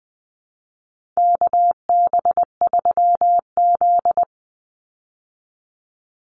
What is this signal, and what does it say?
Morse code, heard as a keyed tone, about 20 wpm, 700 Hz. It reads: KB3Z
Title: Call Signs Repeated 2x - 20wpm